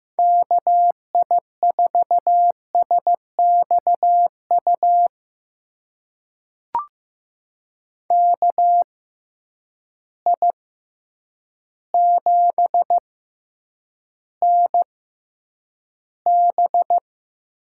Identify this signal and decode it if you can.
Morse code, heard as a keyed tone, about 15 wpm, spaced as for 8 wpm, 700 Hz. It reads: KI4SXU KI7NB
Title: USA Call Signs - 6 Characters - 6x Character Spacing - 15wpm